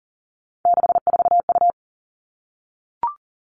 Morse code, heard as a keyed tone, about 42 wpm, 700 Hz. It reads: 64U
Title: Three Challenging Letter-Number Combinations 40wpm